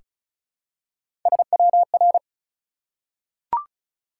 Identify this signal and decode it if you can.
Morse code, heard as a keyed tone, about 36 wpm, 700 Hz. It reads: SWR
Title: Top 180 QSO Elements 35wpm